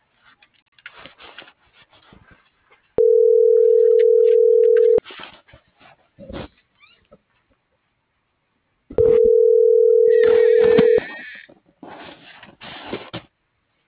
Ringback tone (North America), 440 + 480 Hz.